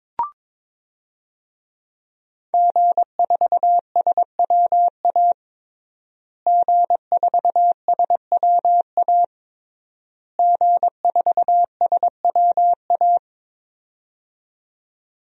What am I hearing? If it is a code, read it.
Morse code, heard as a keyed tone, about 22 wpm, 700 Hz. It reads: G4SWA G4SWA G4SWA
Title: DXCC-100 Call Signs - 5 Characters - Repeated 3x - 22wpm